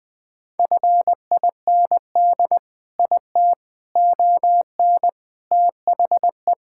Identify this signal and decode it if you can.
Morse code, heard as a keyed tone, about 20 wpm, 700 Hz. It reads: FIND IT ON THE
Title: Sentences with 4-Word Phrases - 20wpm